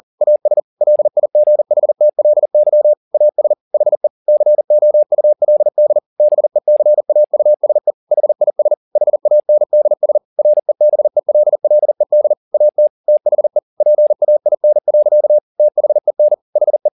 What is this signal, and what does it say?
Morse code, heard as a keyed tone, about 40 wpm, 600 Hz. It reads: AS LIGHTLY AS HE COULD BECAUSE HIS HANDS REBELLED AT THE PAIN. THEN HE
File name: the old man and the sea 2 40wpm